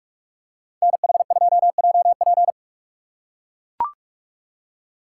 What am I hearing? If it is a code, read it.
Morse code, heard as a keyed tone, about 43 wpm, 700 Hz. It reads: NH2JP